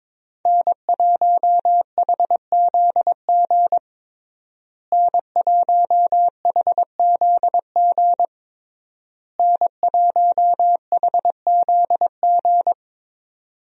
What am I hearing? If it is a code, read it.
Morse code, heard as a keyed tone, about 22 wpm, 700 Hz. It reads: N1HZG N1HZG N1HZG
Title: USA Call Signs - 5 Characters - Repeated 3x - 22wpm